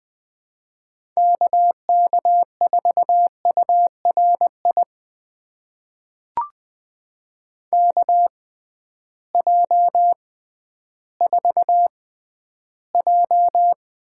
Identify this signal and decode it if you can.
Morse code, heard as a keyed tone, about 20 wpm, spaced as for 12 wpm, 700 Hz. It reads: KK4URI KJ4J
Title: USA Call Signs - 6x Character Spacing - 20wpm